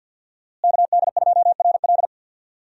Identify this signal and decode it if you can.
Morse code, heard as a keyed tone, about 47 wpm, 700 Hz. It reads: KD2RL
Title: USA Call Signs - 5 Characters - Repeated 2x - 50wpm